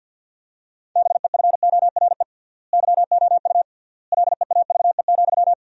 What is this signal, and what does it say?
Morse code, heard as a keyed tone, about 50 wpm, 700 Hz. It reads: BEFORE YOU LEAVE,
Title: 3x Word-Spacing - Sentences from Top 400 Words - 50wpm